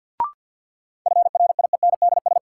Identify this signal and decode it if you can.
Morse code, heard as a keyed tone, about 53 wpm, 700 Hz. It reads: FRIENDS
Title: Sentences from Top 1000 Words - Review Part 5 - New Word Drill - 50wpm